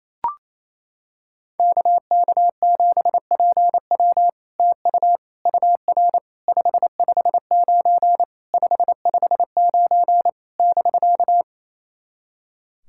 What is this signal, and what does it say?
Morse code, heard as a keyed tone, about 27 wpm, 700 Hz. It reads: KK7PW TU UR 559 559 <BK>
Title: SOTA QSO Practice 28wpm